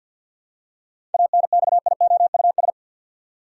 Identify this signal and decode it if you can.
Morse code, heard as a keyed tone, about 50 wpm, 700 Hz. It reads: ANXIOUS
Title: Seven Letter Words - 50wpm